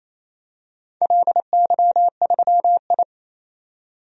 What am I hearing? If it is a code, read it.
Morse code, heard as a keyed tone, about 28 wpm, 700 Hz. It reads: LY3I
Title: DXCC-100 Call Signs - 4 Characters - 28wpm